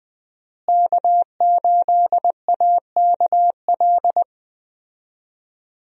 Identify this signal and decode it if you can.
Morse code, heard as a keyed tone, about 20 wpm, 700 Hz. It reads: K8AKL
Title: USA Call Signs - 5 Characters - Repeated 2x - 20wpm